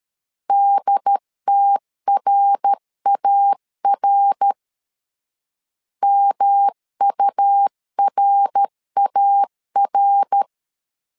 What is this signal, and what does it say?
Morse code, heard as a keyed tone, about 13 wpm, 800 Hz. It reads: DTRAR MURAR
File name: LEZIONI / 3 MWRO / 3 MWR 12wpm 1711